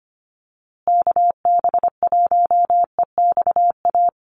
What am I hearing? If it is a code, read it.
Morse code, heard as a keyed tone, about 25 wpm, 700 Hz. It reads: KB1EXA
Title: USA Call Signs Repeated 3x 25wpm